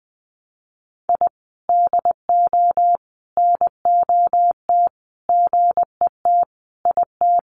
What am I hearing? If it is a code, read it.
Morse code, heard as a keyed tone, about 19 wpm, 700 Hz. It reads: I DO NOT GET IT
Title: Sets of 5 Words Encore 20wpm